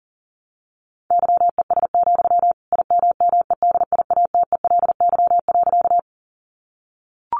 Morse code, heard as a keyed tone, about 39 wpm, 700 Hz. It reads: YES, IMMEDIATELY.
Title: Sentences from Top 1100 Words 40wpm